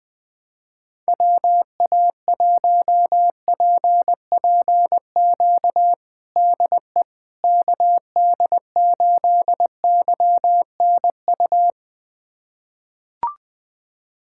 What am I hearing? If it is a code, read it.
Morse code, heard as a keyed tone, about 21 wpm, 700 Hz. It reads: WA1PPQ DE KD8YNU
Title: USA Callsign DE Callsign - 6 Characters - 20wpm